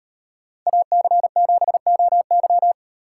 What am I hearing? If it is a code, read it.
Morse code, heard as a keyed tone, about 33 wpm, 700 Hz. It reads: AC7OY